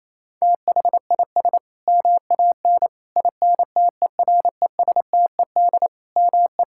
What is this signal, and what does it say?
Morse code, heard as a keyed tone, about 27 wpm, 700 Hz. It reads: THIS MAN INTERESTED ME
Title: Sentences from Top 1000 Words - Review Part 6 - 28wpm